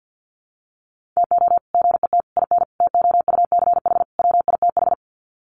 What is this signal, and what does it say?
Morse code, heard as a keyed tone, about 50 wpm, 700 Hz. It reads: TO GET IN TOUCH WITH
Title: Sets of 5 Words Encore2 50wpm